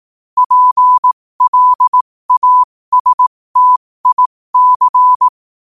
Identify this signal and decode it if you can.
Morse code, heard as a keyed tone, about 18 wpm, spaced as for 15 wpm, 1000 Hz. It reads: PLASTIC
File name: qotdfiles / 15wpm